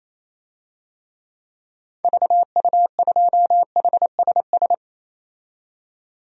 Morse code, heard as a keyed tone, about 28 wpm, 700 Hz. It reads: VU2HSS
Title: DXCC-100 Call Signs - 6 Characters - 28wpm